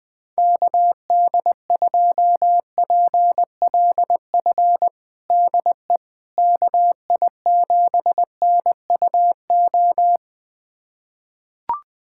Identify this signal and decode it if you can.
Morse code, heard as a keyed tone, about 20 wpm, 700 Hz. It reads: KD2PLF DE KI7NUO
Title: USA Callsign DE Callsign - 6 Characters - 20wpm